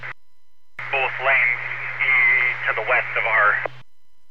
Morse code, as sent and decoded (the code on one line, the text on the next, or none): none